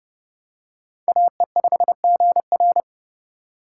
Morse code, heard as a keyed tone, about 30 wpm, 700 Hz. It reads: AE5GR